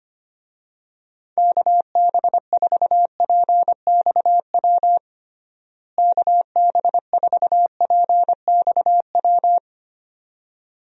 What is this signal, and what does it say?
Morse code, heard as a keyed tone, about 25 wpm, 700 Hz. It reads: KB4PXW KB4PXW
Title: USA Call Signs - 6 Characters - Repeated 2x - 25wpm